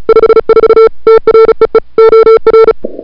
Morse code, heard as a keyed tone, about 36 wpm, 1340 Hz, with noise. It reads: 54 TREE OR
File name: n6tr / Fab2007CWSprintHour1